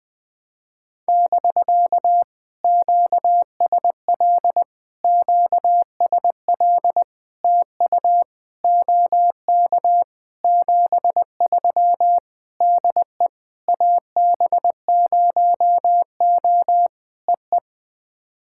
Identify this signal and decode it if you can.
Morse code, heard as a keyed tone, about 20 wpm, 700 Hz. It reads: <BK> QSL QSL TU OK 73 DE AB0O EE
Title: POTA QSO Practice - 20wpm